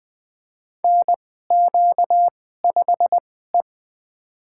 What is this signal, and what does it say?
Morse code, heard as a keyed tone, about 20 wpm, spaced as for 14 wpm, 700 Hz. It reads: NQ5E
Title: USA Call Signs - 2x Character Spacing - 20wpm